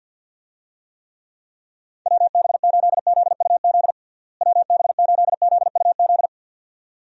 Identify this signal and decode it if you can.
Morse code, heard as a keyed tone, about 51 wpm, 700 Hz. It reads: WB8ZUZ WB8ZUZ
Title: USA Call Signs - 6 Characters - Repeated 2x - 50wpm